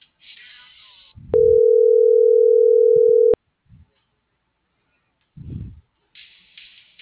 Ringback tone (North America), 440 + 480 Hz.